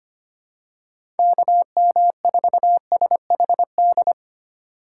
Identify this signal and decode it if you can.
Morse code, heard as a keyed tone, about 25 wpm, 700 Hz. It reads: KM4SHD
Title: USA Call Signs - 3x Character Spacing - 25wpm